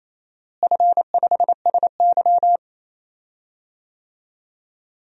Morse code, heard as a keyed tone, about 28 wpm, 700 Hz. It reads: F5SY
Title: DXCC-100 Call Signs - 4 Characters - Repeated 3x - 28wpm